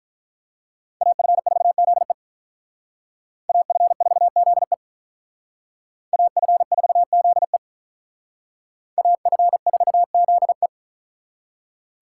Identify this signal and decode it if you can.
Morse code, heard as a keyed tone, about 42 wpm, 700 Hz. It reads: AF4ZE AF4ZE AF4ZE AF4ZE
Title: Speed Racing - USA 2x2 Call Signs 35wpm